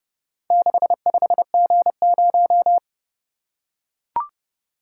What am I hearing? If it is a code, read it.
Morse code, heard as a keyed tone, about 28 wpm, 700 Hz. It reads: 65G0